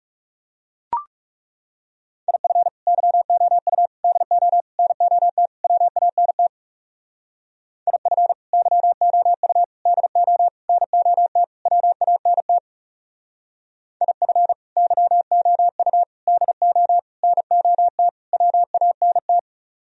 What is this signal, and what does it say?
Morse code, heard as a keyed tone, about 38 wpm, 700 Hz. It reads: IF YOU DO NOT WANT IF YOU DO NOT WANT IF YOU DO NOT WANT